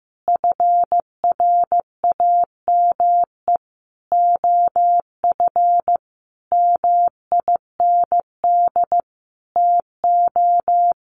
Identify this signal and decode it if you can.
Morse code, heard as a keyed tone, about 15 wpm, 700 Hz. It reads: FRAME OF MIND TO
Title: Sentences from Top 1400 Words 15wpm